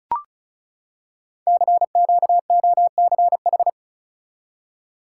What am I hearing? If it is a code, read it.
Morse code, heard as a keyed tone, about 36 wpm, 690 Hz. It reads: CQOCH
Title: Code Groups - 5 Letter:Numbers - 35wpm